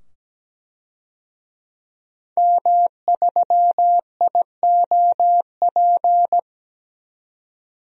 Morse code, heard as a keyed tone, about 17 wpm, 700 Hz. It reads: M3IOP
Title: DXCC-100 Call Signs - 5 Characters - 17wpm